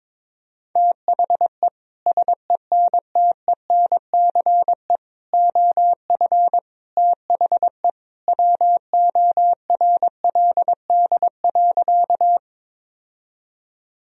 Morse code, heard as a keyed tone, about 22 wpm, 700 Hz. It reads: THE SENTENCE OF THE WORLD.
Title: Sentences from Top 1000 Words - Review Part 1 - 22wpm